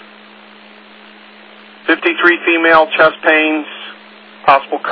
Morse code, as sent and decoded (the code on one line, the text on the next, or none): none